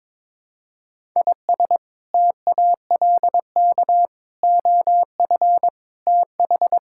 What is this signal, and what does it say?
Morse code, heard as a keyed tone, about 22 wpm, 700 Hz. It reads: IS TALK OF TH
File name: practice / 22wpm / Build A Phrase - 5-Word Phrases - 22wpm